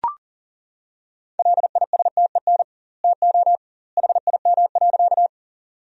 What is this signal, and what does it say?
Morse code, heard as a keyed tone, about 40 wpm, 700 Hz. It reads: LISTEN TO HIM.
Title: 2x Word-Spacing - Sentences from Top 400 Words - 40wpm